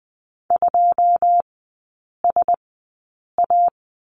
Morse code, heard as a keyed tone, about 20 wpm, spaced as for 8 wpm, 700 Hz. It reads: 2SA
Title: 152 Lesson 25e - Letter p - Three Characters 20wpm